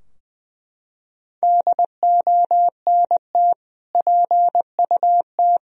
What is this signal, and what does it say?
Morse code, heard as a keyed tone, about 20 wpm, 700 Hz. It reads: DONT PUT
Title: Sentences from Top 1000 Words - Review Part 2 - 20wpm